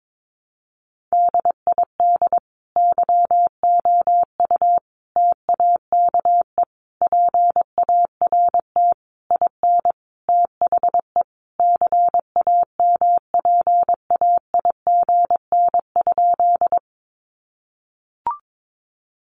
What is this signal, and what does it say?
Morse code, heard as a keyed tone, about 22 wpm, 700 Hz. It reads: DID YOU TAKE PART IN THE CAMPAIGN?
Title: Sentences from Top 1200 Words 22wpm